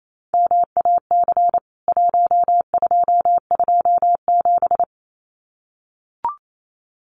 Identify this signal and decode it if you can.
Morse code, heard as a keyed tone, about 28 wpm, 700 Hz. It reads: MAC 1227
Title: CWT Exchange 28wpm